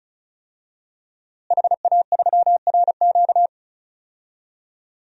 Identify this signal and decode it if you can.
Morse code, heard as a keyed tone, about 34 wpm, 700 Hz. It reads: HA3RQ
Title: DXCC-100 Call Signs - 5 Characters - Repeated 2x - 35wpm